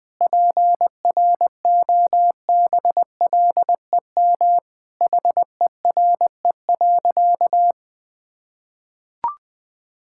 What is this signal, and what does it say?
Morse code, heard as a keyed tone, about 20 wpm, 700 Hz. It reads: PROBLEM HERE.
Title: Sentences with 5-Word Phrases - Spoken Beginning - 20wpm